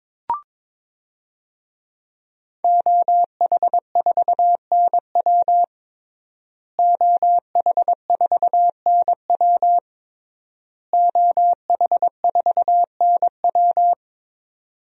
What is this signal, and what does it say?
Morse code, heard as a keyed tone, about 22 wpm, 700 Hz. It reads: OH4NW OH4NW OH4NW
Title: DXCC-100 Call Signs - 5 Characters - Repeated 3x - 22wpm